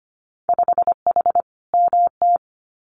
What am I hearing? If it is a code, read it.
Morse code, heard as a keyed tone, about 25 wpm, 700 Hz. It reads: 5H MT
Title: Winter Field Day Exchanges Encore 25wpm